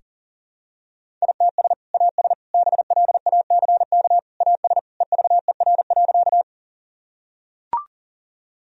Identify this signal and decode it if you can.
Morse code, heard as a keyed tone, about 40 wpm, 700 Hz. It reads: ITS AS BLACK AS EVER.
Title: Sentences from Top 1000 Words - Review Part 2 - 40wpm